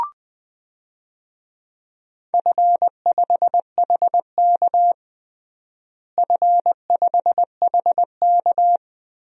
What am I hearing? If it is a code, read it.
Morse code, heard as a keyed tone, about 20 wpm, 700 Hz. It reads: F5HK F5HK
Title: DXCC-100 Call Signs - 4 Characters - Repeated 2x - 20wpm